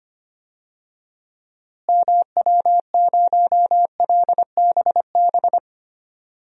Morse code, heard as a keyed tone, about 25 wpm, 700 Hz. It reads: MW0LBB
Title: DXCC-100 Call Signs - 6 Characters - Repeated 2x - 25wpm